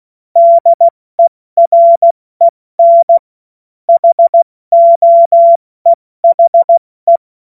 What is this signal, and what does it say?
Morse code, heard as a keyed tone, about 16 wpm, 680 Hz. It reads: DEREN HOEHE